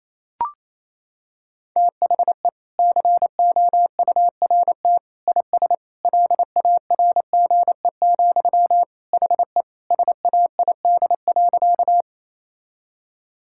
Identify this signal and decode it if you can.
Morse code, heard as a keyed tone, about 28 wpm, 700 Hz. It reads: THE COURT IS LARGE, HE SAID.
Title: Sentences from Top 1600 Words 28wpm